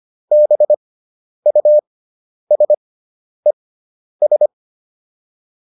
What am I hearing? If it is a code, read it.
Morse code, heard as a keyed tone, about 25 wpm, spaced as for 10 wpm, 600 Hz. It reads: BUSES